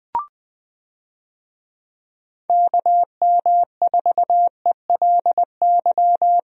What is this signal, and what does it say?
Morse code, heard as a keyed tone, about 20 wpm, 700 Hz. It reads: KM4ELY
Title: USA Call Signs - 6 Characters - Repeated 2x - 20wpm